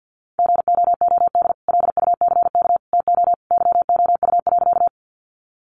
Morse code, heard as a keyed tone, about 50 wpm, 700 Hz. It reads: GOOD LUCK TO YOU.
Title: Sentences from Top 1500 Words 50wpm